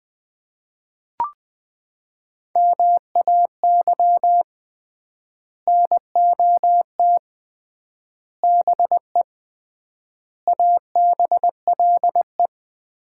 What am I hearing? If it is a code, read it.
Morse code, heard as a keyed tone, about 20 wpm, 700 Hz. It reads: MAY NOT BE ABLE
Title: 3x Word-Spacing - Sets of 5 Words - 20wpm